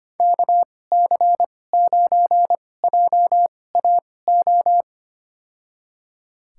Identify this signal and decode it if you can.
Morse code, heard as a keyed tone, about 25 wpm, spaced as for 18 wpm, 700 Hz. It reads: KC9JAO
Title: USA Call Signs - 6 Characters - 2x Character Spacing - 25wpm